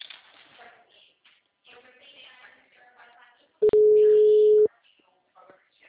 Ringback tone (Europe, 425 Hz).
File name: out- -2006- - - .30262